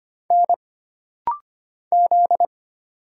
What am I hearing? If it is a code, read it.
Morse code, heard as a keyed tone, about 25 wpm, 700 Hz. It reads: N Z
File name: Single Character (Letter:Number:Punctuation:Prosign) - Rapid-Fire - 25wpm